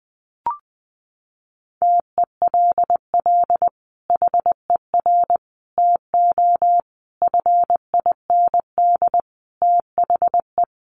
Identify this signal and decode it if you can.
Morse code, heard as a keyed tone, about 20 wpm, 700 Hz. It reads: TELL HER TO FIND THE
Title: Sentences from Top 1200 Words 20wpm